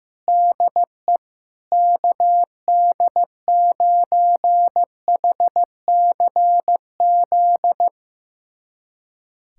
Morse code, heard as a keyed tone, about 16 wpm, 700 Hz. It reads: DE KD9HCZ
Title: USA Callsign DE Callsign - 6 Characters - 15wpm